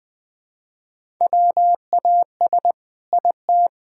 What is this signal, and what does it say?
Morse code, heard as a keyed tone, about 20 wpm, 700 Hz. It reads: WAS IT
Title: Sentences from Top 1000 Words - Review Part 6 - 20wpm